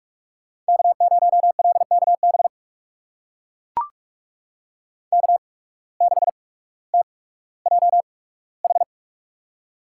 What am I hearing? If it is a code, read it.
Morse code, heard as a keyed tone, about 44 wpm, spaced as for 19 wpm, 700 Hz. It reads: K0LKB K6TJH
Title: USA Call Signs - 8x Character Spacing - 45wpm